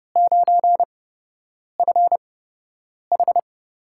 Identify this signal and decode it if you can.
Morse code, heard as a keyed tone, about 30 wpm, spaced as for 8 wpm, 700 Hz. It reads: 9FH
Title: USA Call Signs - 5 Characters - 8x Character Spacing - 30wpm